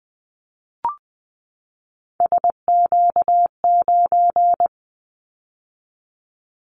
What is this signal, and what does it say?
Morse code, heard as a keyed tone, about 20 wpm, 700 Hz. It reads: SQ9
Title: Callsign Prefixes for Top 25 DXCC - 20wpm